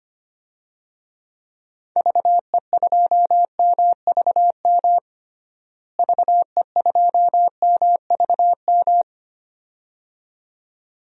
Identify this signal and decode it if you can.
Morse code, heard as a keyed tone, about 25 wpm, 700 Hz. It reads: VE2MVM VE2MVM
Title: DXCC-100 Call Signs - 6 Characters - Repeated 2x - 25wpm